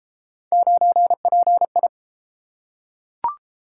Morse code, heard as a keyed tone, about 33 wpm, 700 Hz. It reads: 9PI